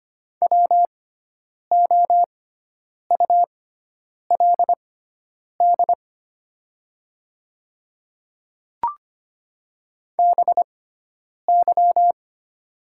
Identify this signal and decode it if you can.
Morse code, heard as a keyed tone, about 25 wpm, spaced as for 9 wpm, 700 Hz. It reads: WOULD BY
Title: 6x Character Spacing - Top 100 Words - 25wpm